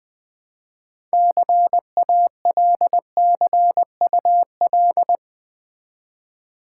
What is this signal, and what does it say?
Morse code, heard as a keyed tone, about 20 wpm, 700 Hz. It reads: CALCUL
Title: Build A Word - 3 Letters At A Time - 9-Letter Words - 20wpm